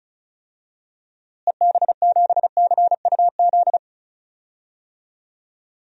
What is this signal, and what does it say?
Morse code, heard as a keyed tone, about 34 wpm, 700 Hz. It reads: EB7CUZ
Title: DXCC-100 Call Signs - 6 Characters - Repeated 2x - 35wpm